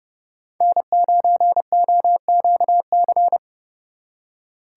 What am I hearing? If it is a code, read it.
Morse code, heard as a keyed tone, about 38 wpm, 700 Hz. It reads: N9OQC